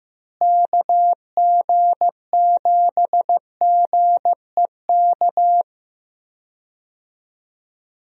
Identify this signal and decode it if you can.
Morse code, heard as a keyed tone, about 15 wpm, 700 Hz. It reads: KG7GEK
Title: USA Call Signs - 6 Characters - Repeated 2x - 15wpm